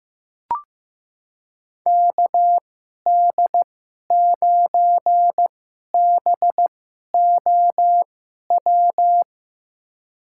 Morse code, heard as a keyed tone, about 15 wpm, spaced as for 11 wpm, 700 Hz. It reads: KD9BOW
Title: USA Call Signs - 6 Characters - 2x Character Spacing - 15wpm